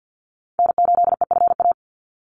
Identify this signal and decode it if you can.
Morse code, heard as a keyed tone, about 50 wpm, 700 Hz. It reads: N8EFA
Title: USA Call Signs 50wpm